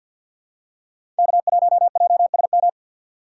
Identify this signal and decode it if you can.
Morse code, heard as a keyed tone, about 47 wpm, 700 Hz. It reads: K1JSM
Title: USA Call Signs - 5 Characters - 6x Character Spacing - 50wpm